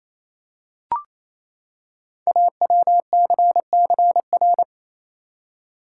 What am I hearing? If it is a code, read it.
Morse code, heard as a keyed tone, about 28 wpm, 700 Hz. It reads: AWCCR
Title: Code Groups - 5 Letters - 28wpm